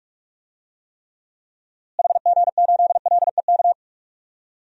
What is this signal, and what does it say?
Morse code, heard as a keyed tone, about 46 wpm, 700 Hz. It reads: HG8LEK
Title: DXCC-100 Call Signs - 6 Characters - Repeated 2x - 45wpm